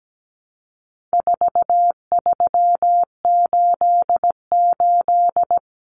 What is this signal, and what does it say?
Morse code, heard as a keyed tone, about 17 wpm, 700 Hz. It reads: 4388
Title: Common Cut Numbers - 4-Digit 17wpm